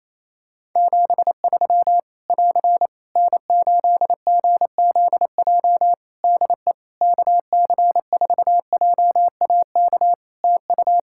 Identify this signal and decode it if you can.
Morse code, heard as a keyed tone, about 28 wpm, 700 Hz. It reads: 73 <AR> N8GZJ DE KC4JAK TU
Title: Practice QSO - Exchange 4 - Ending - 28wpm